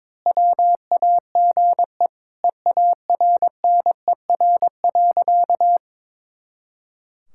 Morse code, heard as a keyed tone, about 21 wpm, 700 Hz. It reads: WAGE EARNER.
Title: Pangram Challenge v2 - 22wpm